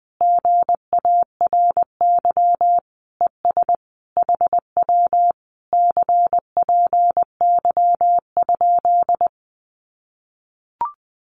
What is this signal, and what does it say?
Morse code, heard as a keyed tone, about 20 wpm, 700 Hz. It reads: GARY ES HW CPY?
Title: Common QSO Phrases Encore 20wpm